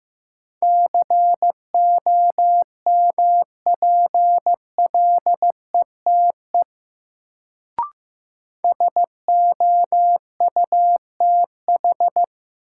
Morse code, heard as a keyed tone, about 15 wpm, 700 Hz. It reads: COMPLETE SOUTH
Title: Sentences from Top 400 Words - New Word Drill - 15wpm